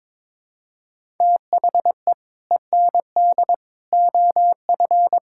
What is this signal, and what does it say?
Morse code, heard as a keyed tone, about 22 wpm, 700 Hz. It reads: THE END OF
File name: practice / 22wpm / Build A Phrase - 5-Word Phrases - 22wpm